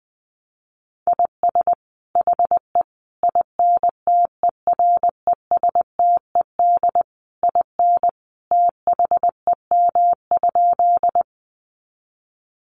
Morse code, heard as a keyed tone, about 20 wpm, 700 Hz. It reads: IS HE INTERESTED IN THEM?
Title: Sentences from Top 900 Words 20wpm